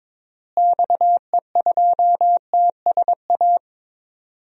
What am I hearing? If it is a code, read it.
Morse code, heard as a keyed tone, about 22 wpm, 700 Hz. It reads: XE2TSA
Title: DXCC-100 Call Signs - 6 Characters - Repeated 2x - 22wpm